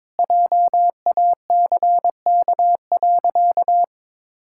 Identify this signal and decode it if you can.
Morse code, heard as a keyed tone, about 22 wpm, 700 Hz. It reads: JACK.
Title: Pangram Challenge v2 - 22wpm